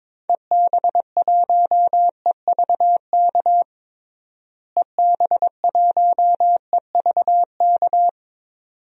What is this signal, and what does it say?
Morse code, heard as a keyed tone, about 22 wpm, 700 Hz. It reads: EB1EVK EB1EVK
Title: DXCC-100 Call Signs - 6 Characters - Repeated 2x - 22wpm